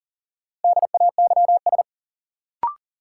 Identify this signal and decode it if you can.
Morse code, heard as a keyed tone, about 40 wpm, 700 Hz. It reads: DAYS
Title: Sentences from Top 1000 Words - Review Part 5 - New Word Drill - 40wpm